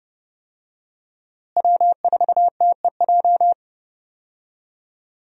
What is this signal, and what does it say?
Morse code, heard as a keyed tone, about 30 wpm, 700 Hz. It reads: W4TEJ
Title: USA Call Signs - 5 Characters - Repeated 3x - 30wpm